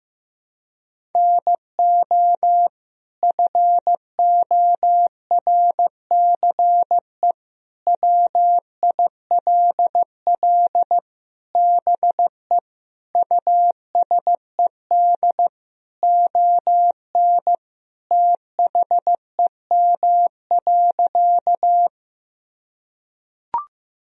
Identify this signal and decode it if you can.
Morse code, heard as a keyed tone, about 15 wpm, 700 Hz. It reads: NO FORCE WILL BE USED ON THEM.
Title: Sentences from Top 1000 Words - Review Part 3 - 15wpm